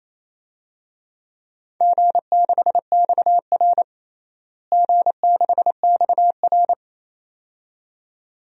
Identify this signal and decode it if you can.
Morse code, heard as a keyed tone, about 28 wpm, 700 Hz. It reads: G6XR G6XR
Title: DXCC-100 Call Signs - 4 Characters - Repeated 2x - 28wpm